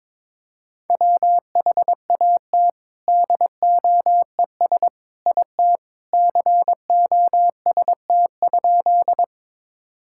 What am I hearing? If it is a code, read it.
Morse code, heard as a keyed tone, about 22 wpm, 700 Hz. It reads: WHAT DOES IT COST?
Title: Sentences from Top 1000 Words - Review Part 5 - 22wpm